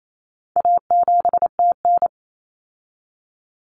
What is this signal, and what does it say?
Morse code, heard as a keyed tone, about 28 wpm, 700 Hz. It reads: A7TN
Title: Common Cut Numbers - 4-Digit 28wpm